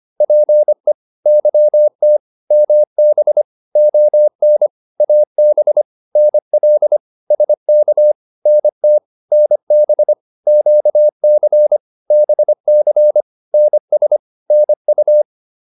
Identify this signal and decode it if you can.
Morse code, heard as a keyed tone, about 25 wpm, 600 Hz. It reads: PE YT MB ON AB NL SK NT NB QC BC NS NU